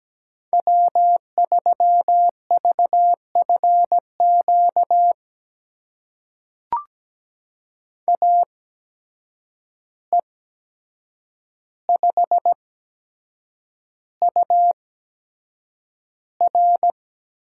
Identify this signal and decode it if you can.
Morse code, heard as a keyed tone, about 17 wpm, spaced as for 7 wpm, 700 Hz. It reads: W3VFQ AE5UR
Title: USA Call Signs - 5 Characters - 8x Character Spacing - 17wpm